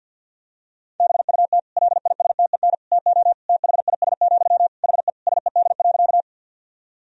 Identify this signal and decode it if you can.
Morse code, heard as a keyed tone, about 51 wpm, 700 Hz. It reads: BUT LISTEN TO THIS, HE SED.
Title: Sentences from Top 400 Words and Abbreviations 50wpm